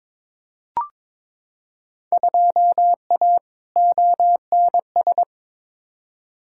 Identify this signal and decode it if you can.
Morse code, heard as a keyed tone, about 22 wpm, 700 Hz. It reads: 2A ONS
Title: Field Day Exchanges Encore - 22wpm